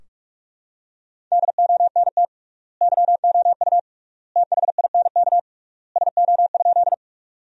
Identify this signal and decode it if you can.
Morse code, heard as a keyed tone, about 46 wpm, 700 Hz. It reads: DONT YOU THINK SO?
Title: 3x Word-Spacing - Sentences from Top 300 Words - 45wpm